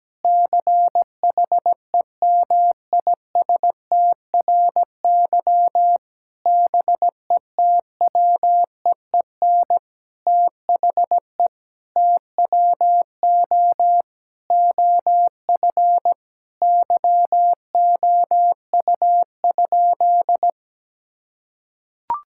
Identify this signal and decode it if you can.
Morse code, heard as a keyed tone, about 17 wpm, 700 Hz. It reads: CHEMISTRY BETWEEN THE TWO OF YOU?
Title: Sentences from Top 1000 Words - Review Part 6 - 17wpm